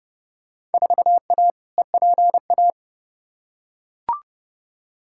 Morse code, heard as a keyed tone, about 30 wpm, 700 Hz. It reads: 4A EPA
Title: Field Day Exchanges - 30wpm